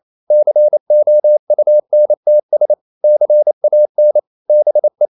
Morse code, heard as a keyed tone, about 28 wpm, 600 Hz. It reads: COUNTS CAN BE
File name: CW / Practice / EnglishQuotations0128